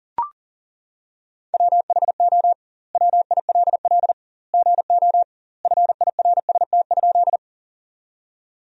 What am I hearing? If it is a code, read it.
Morse code, heard as a keyed tone, about 40 wpm, 700 Hz. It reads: WHO WILL GO FIRST?
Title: 2x Word-Spacing - Sentences from Top 100 Words - 40wpm